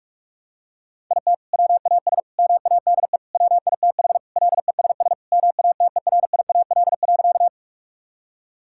Morse code, heard as a keyed tone, about 41 wpm, 700 Hz. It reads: IT WAS MADE WITH LESS MATERIAL.
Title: Sentences from Top 1000 Words - Review Part 5 - 45wpm